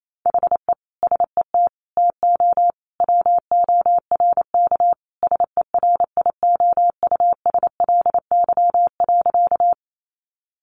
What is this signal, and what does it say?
Morse code, heard as a keyed tone, about 28 wpm, 700 Hz. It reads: HE SET TO WORK SERIOUSLY.
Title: Sentences from Top 1200 Words 28wpm